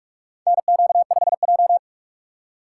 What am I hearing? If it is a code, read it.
Morse code, heard as a keyed tone, about 46 wpm, 700 Hz. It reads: NQ5J
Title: USA Call Signs - 4x Character Spacing - 45wpm